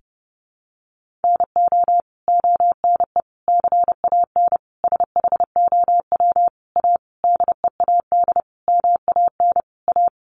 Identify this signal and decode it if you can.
Morse code, heard as a keyed tone, about 28 wpm, 700 Hz. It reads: NO ONE CAN SHOW A DEAD MAN A
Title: Sentences from Top 1500 Words 30wpm